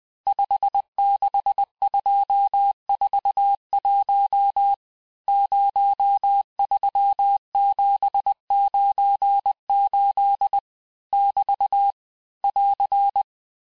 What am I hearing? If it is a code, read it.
Morse code, heard as a keyed tone, about 20 wpm, 800 Hz. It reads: 56241 03798 <BT> <AR>
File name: cours-858-texte-chiffre-vitesse-19mpm